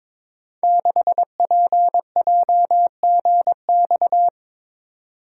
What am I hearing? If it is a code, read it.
Morse code, heard as a keyed tone, about 23 wpm, 700 Hz. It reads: 6PJGX
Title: Code Groups - 5 Letter:Numbers - 22wpm